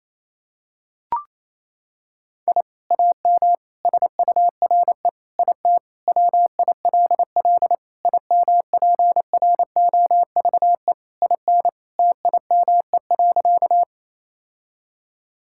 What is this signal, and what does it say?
Morse code, heard as a keyed tone, about 28 wpm, 700 Hz. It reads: I AM SURE IT WILL IMPROVE IN TIME.
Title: Sentences from Top 1000 Words - Review Part 5 - 28wpm